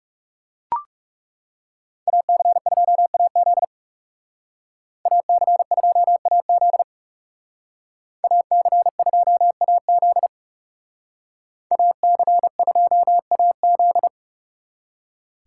Morse code, heard as a keyed tone, about 36 wpm, 700 Hz. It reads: AC2AZ AC2AZ AC2AZ AC2AZ
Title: Speed Racing - USA 2x2 Call Signs 30wpm